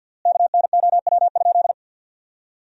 Morse code, heard as a keyed tone, about 50 wpm, 700 Hz. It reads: KNOW?
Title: 3x Word-Spacing - Sentences from Top 200 Words - 50wpm